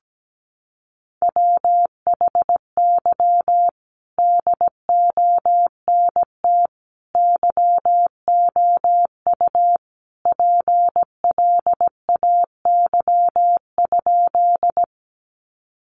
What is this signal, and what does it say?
Morse code, heard as a keyed tone, about 17 wpm, 700 Hz. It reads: WHY DONT YOU PLAY?
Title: Sentences from Top 300 Words 17wpm